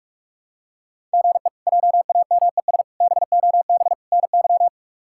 Morse code, heard as a keyed tone, about 41 wpm, 700 Hz. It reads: GE JAMES BOB NY
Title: K1USN SST QSO Practice - 45wpm